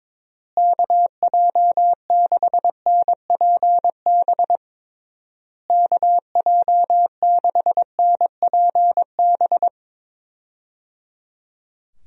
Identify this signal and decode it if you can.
Morse code, heard as a keyed tone, about 22 wpm, 700 Hz. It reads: KJ6NPB KJ6NPB
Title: USA Call Signs - 6 Characters - Repeated 3x - 22wpm